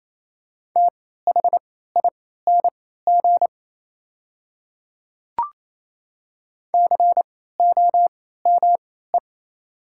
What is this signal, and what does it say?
Morse code, heard as a keyed tone, about 28 wpm, spaced as for 16 wpm, 700 Hz. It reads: THING COME
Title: 3x Character Spacing - Top 100 Words - 28wpm